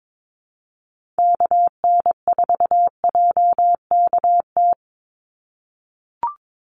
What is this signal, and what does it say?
Morse code, heard as a keyed tone, about 22 wpm, 700 Hz. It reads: KN4JKT
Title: USA Call Signs 22wpm